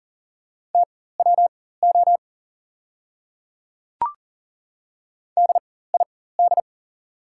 Morse code, heard as a keyed tone, about 38 wpm, spaced as for 18 wpm, 700 Hz. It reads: TWO DID
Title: 4x Character Spacing - Top 300 Words - 40wpm